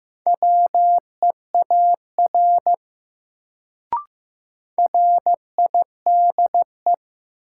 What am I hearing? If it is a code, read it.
Morse code, heard as a keyed tone, about 15 wpm, 700 Hz. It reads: WEAR RIDE
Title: Sentences from Top 1000 Words - Review Part 6 - New Word Drill - 15wpm